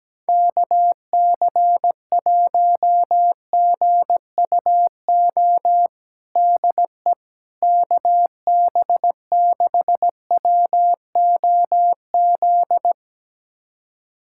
Morse code, heard as a keyed tone, about 17 wpm, 700 Hz. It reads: KC1GUO DE KB6WOZ
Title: USA Callsign DE Callsign 17wpm